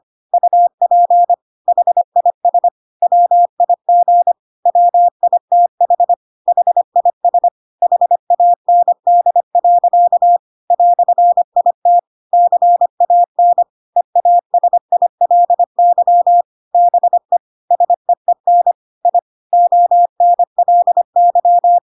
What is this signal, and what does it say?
Morse code, heard as a keyed tone, about 25 wpm, 700 Hz. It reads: UP HIS WIG WITH HIS HAND. "IT CAN EASILY BE SEEN I ONLY
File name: Pinocchio / Pinocchio ch1 25 25 ik1hge